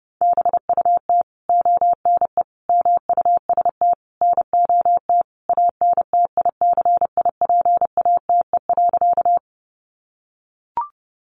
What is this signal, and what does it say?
Morse code, heard as a keyed tone, about 30 wpm, 700 Hz. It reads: BUT ONE MUST NOT ANTICIPATE.
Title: Sentences from Top 1500 Words 30wpm